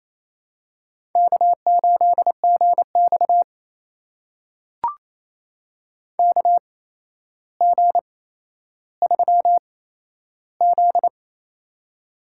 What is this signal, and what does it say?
Morse code, heard as a keyed tone, about 27 wpm, spaced as for 12 wpm, 700 Hz. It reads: K8GX KG3Z
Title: USA Call Signs - 8x Character Spacing - 28wpm